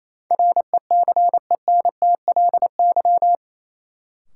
Morse code, heard as a keyed tone, about 28 wpm, 700 Hz. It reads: RECENTLY
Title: Sentences from Top 1200 Words - New Word Drill - 28wpm